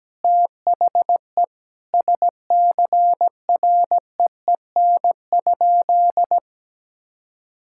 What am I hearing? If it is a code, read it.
Morse code, heard as a keyed tone, about 17 wpm, 700 Hz. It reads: THE SCREEN?
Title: Sentences from Top 1000 Words - Review Part 6 - 17wpm